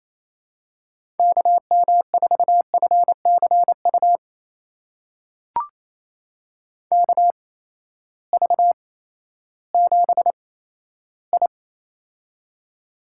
Morse code, heard as a keyed tone, about 28 wpm, spaced as for 14 wpm, 700 Hz. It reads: KM4FCU KV7I
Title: USA Call Signs - 8x Character Spacing - 28wpm